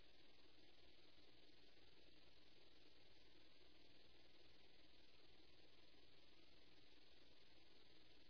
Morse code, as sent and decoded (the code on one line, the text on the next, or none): none